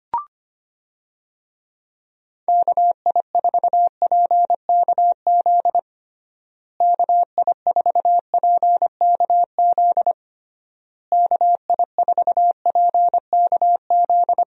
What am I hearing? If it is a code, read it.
Morse code, heard as a keyed tone, about 25 wpm, 700 Hz. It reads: KI4PKZ KI4PKZ KI4PKZ
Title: USA Call Signs - 6 Characters - Repeated 3x - 25wpm